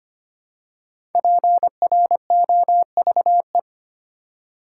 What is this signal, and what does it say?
Morse code, heard as a keyed tone, about 25 wpm, 700 Hz. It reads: PROVE
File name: 5 Letter Words - First 3 Letters Given - 25wpm